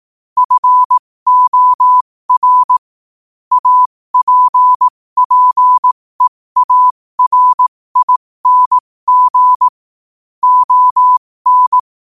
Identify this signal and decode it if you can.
Morse code, heard as a keyed tone, about 18 wpm, spaced as for 15 wpm, 1000 Hz. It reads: FOR APPEARING ON
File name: qotdfiles / 15wpm